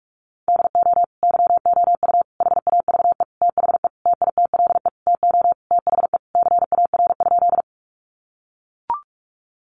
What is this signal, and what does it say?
Morse code, heard as a keyed tone, about 44 wpm, 700 Hz. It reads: DO YOU HAVE THE TITLE TO THE CAR?
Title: Sentences from Top 1200 Words 45wpm